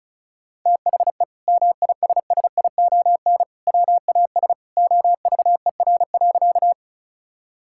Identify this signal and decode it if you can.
Morse code, heard as a keyed tone, about 34 wpm, 700 Hz. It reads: THE MISSION WAS OVER.
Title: Sentences from Top 1600 Words 35wpm